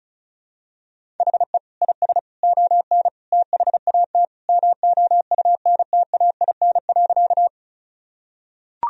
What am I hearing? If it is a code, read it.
Morse code, heard as a keyed tone, about 36 wpm, 690 Hz. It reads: HE IS ON THAT MOUNTAIN.
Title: Sentences from Top 1000 Words - Review Part 1 - 35wpm